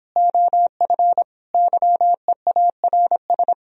Morse code, heard as a keyed tone, about 29 wpm, 700 Hz. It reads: OF YEARS